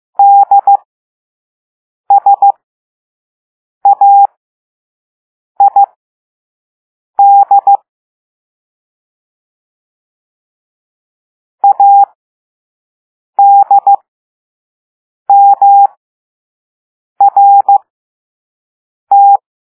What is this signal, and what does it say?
Morse code, heard as a keyed tone, about 15 wpm, spaced as for 5 wpm, 800 Hz. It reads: DSAID ADMRT
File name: Morse / Level3-All-B